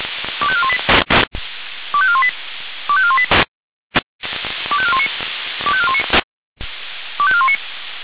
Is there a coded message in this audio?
Yes: five-tone selective call (ZVEI) from 0.4 to 7.6 s, heard as quick runs of five tones, tones of about 70 ms. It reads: 25418 25418 25418 25419 25419 25419